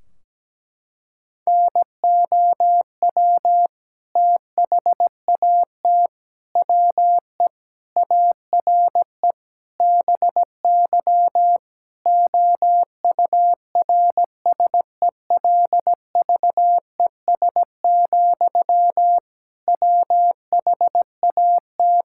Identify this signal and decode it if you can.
Morse code, heard as a keyed tone, about 17 wpm, 700 Hz. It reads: NOW THAT WE ARE BY OURSELVES, WHAT
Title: Sentences from Top 1600 Words 17wpm